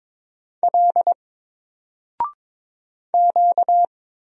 Morse code, heard as a keyed tone, about 22 wpm, 700 Hz. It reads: L Q
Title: Single Character (Letter:Number:Punctuation:Prosign) 22wpm